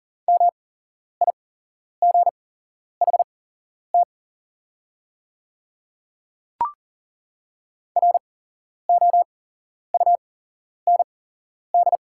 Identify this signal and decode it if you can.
Morse code, heard as a keyed tone, about 40 wpm, spaced as for 11 wpm, 700 Hz. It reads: MIGHT ROUND
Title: 8x Character Spacing - Top 300 Words - 40wpm